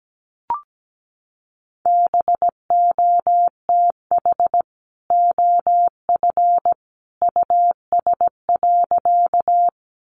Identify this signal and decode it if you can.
Morse code, heard as a keyed tone, about 17 wpm, 700 Hz. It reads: BOTH OF US.
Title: Sentences from Top 300 Words 17wpm